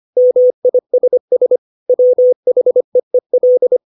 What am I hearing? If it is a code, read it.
Morse code, heard as a keyed tone, about 25 wpm, 500 Hz. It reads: MISS WHEEL
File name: 632 words / 632 Words 25WPM